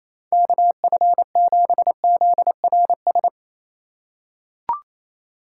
Morse code, heard as a keyed tone, about 28 wpm, 700 Hz. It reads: KF7ZRS
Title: USA Call Signs - 6 Characters - 4x Character Spacing - 28wpm